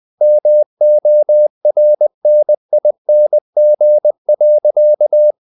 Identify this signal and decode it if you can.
Morse code, heard as a keyed tone, about 20 wpm, 600 Hz. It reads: MORNING.
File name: the old man and the sea part 2 20wpm